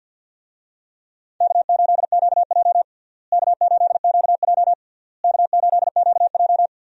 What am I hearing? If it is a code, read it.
Morse code, heard as a keyed tone, about 49 wpm, 700 Hz. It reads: K8QJ K8QJ K8QJ
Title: USA Call Signs - 4 Characters - Repeated 3x - 50wpm